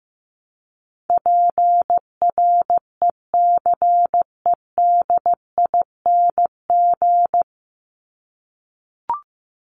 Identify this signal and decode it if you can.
Morse code, heard as a keyed tone, about 15 wpm, 700 Hz. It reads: PRECEDING
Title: ING Suffix - Top 100 Words with Base Words 15wpm